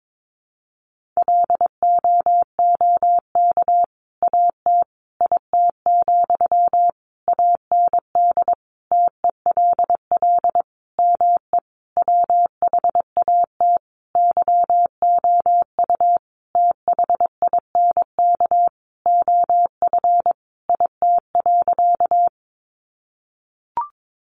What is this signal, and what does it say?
Morse code, heard as a keyed tone, about 22 wpm, 700 Hz. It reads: LOOK AT IT, AND TELL ME WHAT YOU THINK OF IT.
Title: Sentences from Top 200 Words 22wpm